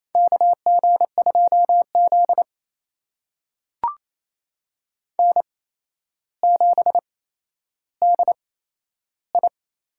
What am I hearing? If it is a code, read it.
Morse code, heard as a keyed tone, about 27 wpm, spaced as for 12 wpm, 700 Hz. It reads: KG2Z N7DI
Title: USA Call Signs - 8x Character Spacing - 28wpm